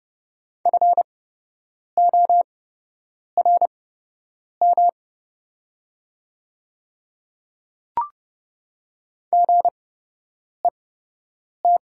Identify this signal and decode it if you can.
Morse code, heard as a keyed tone, about 30 wpm, spaced as for 8 wpm, 700 Hz. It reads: FORM GET
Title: 8x Character Spacing - Top 200 Words - 30wpm